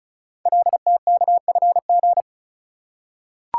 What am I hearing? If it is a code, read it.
Morse code, heard as a keyed tone, about 36 wpm, 700 Hz. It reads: LTKFG
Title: Code Groups - 5 Letters - 35wpm